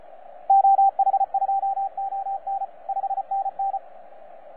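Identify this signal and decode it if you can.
Morse code, heard as a keyed tone, about 34 wpm, 780 Hz, with radio noise. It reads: OH2ON 5NN
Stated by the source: FS/LJ2IJ heard on 12 metres CW, from DX World.net.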